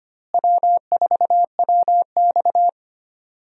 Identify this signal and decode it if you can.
Morse code, heard as a keyed tone, about 25 wpm, 700 Hz. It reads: W4WX
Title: USA Call Signs - 4 Characters - Repeated 3x - 25wpm